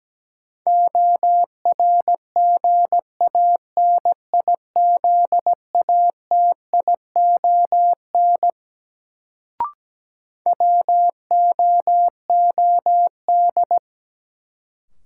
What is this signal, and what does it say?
Morse code, heard as a keyed tone, about 17 wpm, 700 Hz. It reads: ORGANIZATION WOOD
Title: Sentences from Top 1100 Words - New Word Drill Alt - 17wpm